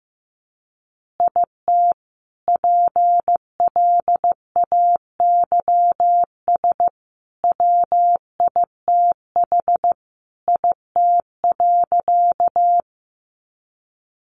Morse code, heard as a keyed tone, about 15 wpm, 700 Hz. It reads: IT PLAYS WITH IT.
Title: Additional Easy Phrases 15wpm